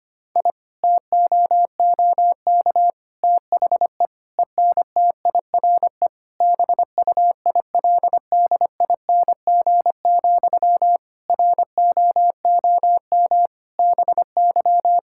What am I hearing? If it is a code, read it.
Morse code, heard as a keyed tone, about 25 wpm, 700 Hz. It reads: I TOOK THE ENTIRE BUILDING, ROOM BY
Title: Sentences from Top 1000 Words - Review Part 5 - 25wpm